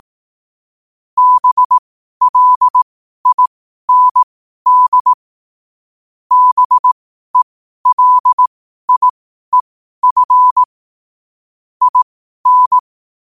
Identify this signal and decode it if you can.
Morse code, heard as a keyed tone, about 18 wpm, spaced as for 12 wpm, 1000 Hz. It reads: BLIND BELIEF IN
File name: qotdfiles / 12wpm